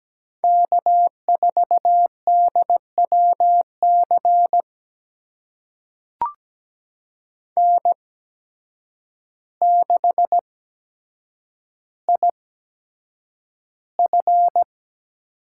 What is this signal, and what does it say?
Morse code, heard as a keyed tone, about 17 wpm, spaced as for 8 wpm, 700 Hz. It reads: K4DWC N6IF
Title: USA Call Signs - 5 Characters - 8x Character Spacing - 17wpm